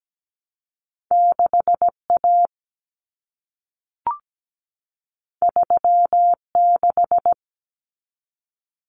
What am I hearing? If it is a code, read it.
Morse code, heard as a keyed tone, about 17 wpm, 700 Hz. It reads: 6A 36
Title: Common Cut Numbers - 2-Digit 17wpm